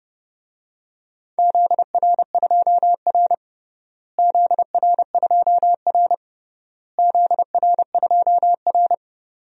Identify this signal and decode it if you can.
Morse code, heard as a keyed tone, about 31 wpm, 700 Hz. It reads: ZR2R ZR2R ZR2R
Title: DXCC-100 Call Signs - 4 Characters - Repeated 3x - 30wpm